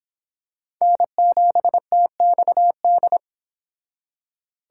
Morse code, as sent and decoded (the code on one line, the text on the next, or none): -. --... - -..- -..
N7TXD